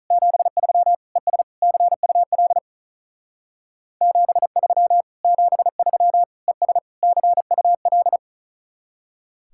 Morse code, heard as a keyed tone, about 44 wpm, 700 Hz. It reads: 73 ES CUL 73 73 ES CUL